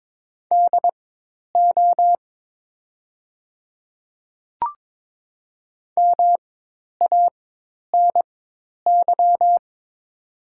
Morse code, heard as a keyed tone, about 22 wpm, spaced as for 10 wpm, 700 Hz. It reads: DO MANY
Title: 4x Character Spacing - Top 200 Words - 22wpm